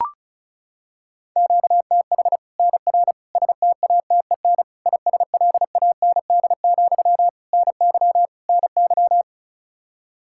Morse code, heard as a keyed tone, about 36 wpm, 700 Hz. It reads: QTH NR STATEN ISLAND, NY NY
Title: Practice QSO - Exchange 1 - Beginning - 35wpm